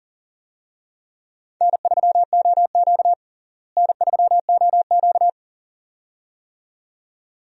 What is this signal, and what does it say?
Morse code, heard as a keyed tone, about 40 wpm, 700 Hz. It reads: N3OQ N3OQ
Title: USA Call Signs - 4 Characters - Repeated 2x - 40wpm